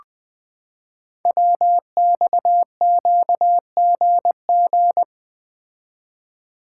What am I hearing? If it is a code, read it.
Morse code, heard as a keyed tone, about 20 wpm, 700 Hz. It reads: WXQGG
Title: Code Groups - 5 Letters - 20wpm